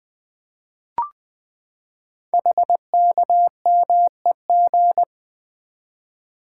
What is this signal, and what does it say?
Morse code, heard as a keyed tone, about 20 wpm, 700 Hz. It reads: HKMEG
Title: Code Groups - 5 Letters - 20wpm